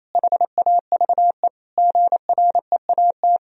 Morse code, heard as a keyed tone, about 28 wpm, 700 Hz. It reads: HAVE GREAT
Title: Sentences from Top 1000 Words - Review Part 4 - 28wpm